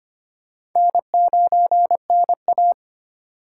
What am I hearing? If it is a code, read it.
Morse code, heard as a keyed tone, about 25 wpm, 700 Hz. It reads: N9NA
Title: USA Call Signs - 4 Characters - 3x Character Spacing - 25wpm